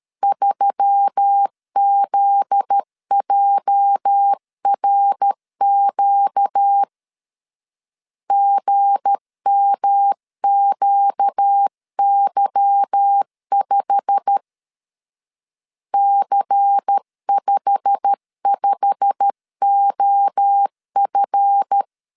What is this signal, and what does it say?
Morse code, heard as a keyed tone, about 13 wpm, 800 Hz. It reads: 3ZJRQ GMQY5 C55OF
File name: LEZIONI / 8 YJ45 / 8 YJ45 12wpm 2151